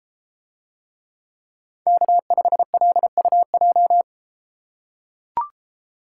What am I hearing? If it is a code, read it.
Morse code, heard as a keyed tone, about 32 wpm, 700 Hz. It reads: K5LUJ